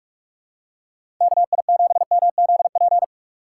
Morse code, heard as a keyed tone, about 45 wpm, 700 Hz. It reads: KI7MZP
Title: USA Call Signs - 4x Character Spacing - 45wpm